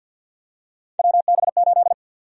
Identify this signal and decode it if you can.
Morse code, heard as a keyed tone, about 51 wpm, 700 Hz. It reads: WB8
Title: USA Call Sign Prefixes - 50wpm